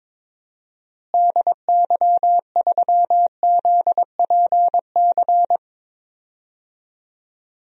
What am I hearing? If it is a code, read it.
Morse code, heard as a keyed tone, about 21 wpm, 700 Hz. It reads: DY3ZPC
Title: DXCC-100 Call Signs - 6 Characters - Repeated 3x - 22wpm